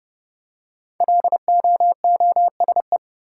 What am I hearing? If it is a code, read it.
Morse code, heard as a keyed tone, about 30 wpm, 700 Hz. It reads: LOOSE
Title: Five Letter Words - 30wpm